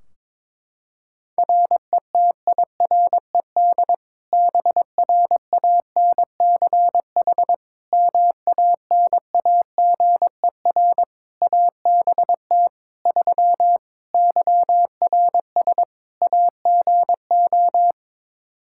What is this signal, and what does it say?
Morse code, heard as a keyed tone, about 22 wpm, 700 Hz. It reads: RETIRED BRANCH MANAGER ABT 3 YRS AGO
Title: Practice QSO - Exchange 3 - More Info - 22wpm